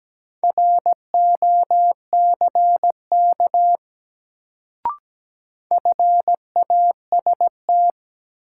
Morse code, heard as a keyed tone, about 17 wpm, 700 Hz. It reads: ROCK FAST
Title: Sentences from Top 400 Words - New Word Drill - 17wpm